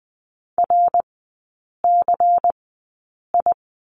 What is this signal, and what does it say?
Morse code, heard as a keyed tone, about 20 wpm, spaced as for 8 wpm, 690 Hz. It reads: RCI
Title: 093 Lesson 16e - Letter c - Three Characters 20wpm